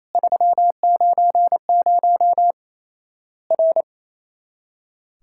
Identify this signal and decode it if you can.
Morse code, heard as a keyed tone, about 28 wpm, 700 Hz. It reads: 390 R
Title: Worked All Europe DX Contest - QTC Exchange - 28wpm